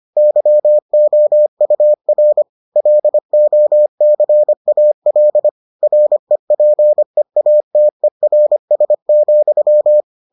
Morse code, heard as a keyed tone, about 25 wpm, 600 Hz. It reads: YOUR LOCAL REPEATERS,